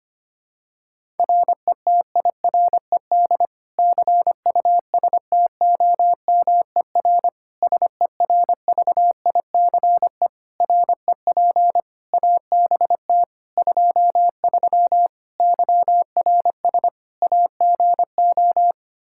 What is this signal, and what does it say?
Morse code, heard as a keyed tone, about 25 wpm, 700 Hz. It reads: RETIRED CUSTOMER SERVICE REP ABT 23 YRS AGO
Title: K1USN SST QSO Practice 25wpm